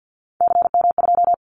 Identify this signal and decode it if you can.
Morse code, heard as a keyed tone, about 50 wpm, 700 Hz. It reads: CM2
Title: Callsign Prefixes for Top 100 DXCC - 50wpm